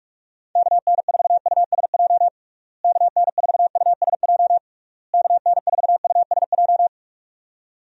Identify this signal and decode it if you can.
Morse code, heard as a keyed tone, about 41 wpm, 700 Hz. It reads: KN4USJ KN4USJ KN4USJ
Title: USA Call Signs - 6 Characters - Repeated 3x - 45wpm